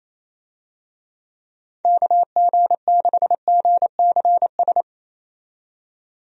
Morse code, heard as a keyed tone, about 28 wpm, 700 Hz. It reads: KG6GCS
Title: USA Call Signs - 6 Characters - 28wpm